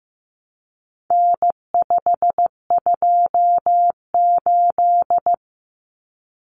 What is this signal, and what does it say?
Morse code, heard as a keyed tone, about 15 wpm, 700 Hz. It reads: N528
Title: Common Cut Numbers - 4-Digit 15wpm